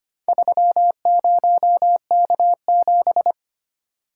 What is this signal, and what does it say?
Morse code, heard as a keyed tone, about 25 wpm, 700 Hz. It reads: 30K7
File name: practice / 25wpm / Four Letter-Number Combinations - 25wpm